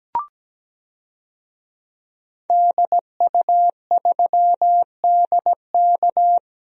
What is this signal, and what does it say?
Morse code, heard as a keyed tone, about 17 wpm, 700 Hz. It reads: DU3DK
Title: DXCC-100 Call Signs - 5 Characters - Repeated 3x - 17wpm